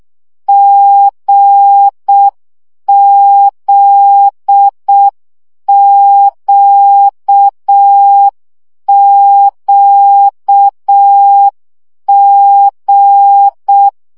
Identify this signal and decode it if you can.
Morse code, heard as a keyed tone, about 6 wpm, 800 Hz. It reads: GZQQG
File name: CW gyakorlas g z q 5 baud